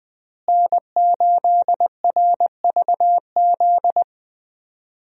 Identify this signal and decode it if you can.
Morse code, heard as a keyed tone, about 21 wpm, 700 Hz. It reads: N8RVZ
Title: USA Call Signs - 5 Characters - Repeated 2x - 20wpm